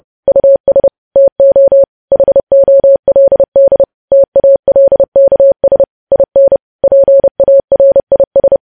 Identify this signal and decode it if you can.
Morse code, heard as a keyed tone, about 30 wpm, 550 Hz. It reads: US TO HOLD TALKS IN PARIS
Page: news headlines in morse code at 30 wpm